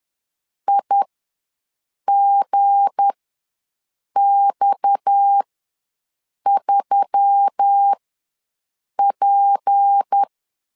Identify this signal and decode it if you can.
Morse code, heard as a keyed tone, about 10 wpm, spaced as for 6 wpm, 800 Hz. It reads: IGX3P
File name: LEZIONI / 7 GF9Z / 7 G ws 10wpm 3843